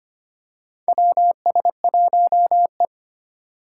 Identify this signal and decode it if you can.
Morse code, heard as a keyed tone, about 25 wpm, 700 Hz. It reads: WS1E
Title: USA Call Signs - 4 Characters - 3x Character Spacing - 25wpm